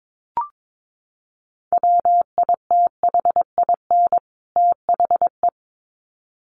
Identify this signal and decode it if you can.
Morse code, heard as a keyed tone, about 22 wpm, 700 Hz. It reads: WITHIN THE
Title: Sets of 2 Words 22wpm